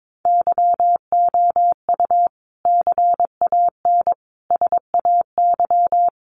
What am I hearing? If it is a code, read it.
Morse code, heard as a keyed tone, about 22 wpm, 700 Hz. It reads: YOU CAN SAY
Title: Sentences from Top 200 Words 22wpm